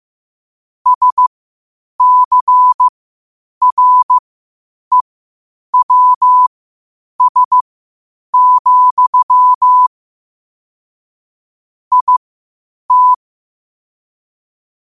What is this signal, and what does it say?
Morse code, heard as a keyed tone, about 15 wpm, spaced as for 8 wpm, 1000 Hz. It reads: SCREWS, IT
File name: qotdfiles / 8wpm